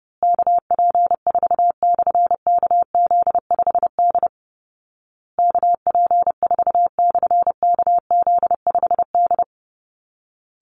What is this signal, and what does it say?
Morse code, heard as a keyed tone, about 28 wpm, 700 Hz. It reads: KP4/KZ5D KP4/KZ5D
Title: Call Signs Repeated 2x - 30wpm